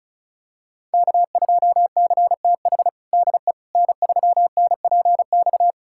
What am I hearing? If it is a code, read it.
Morse code, heard as a keyed tone, about 36 wpm, 700 Hz. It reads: K2CTH DE N3NPX
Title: USA Callsign DE Callsign - 5 Characters - 35wpm